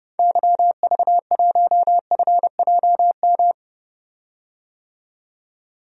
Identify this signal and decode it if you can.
Morse code, heard as a keyed tone, about 30 wpm, 700 Hz. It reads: YV1FJM
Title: DXCC-100 Call Signs - 6 Characters - 30wpm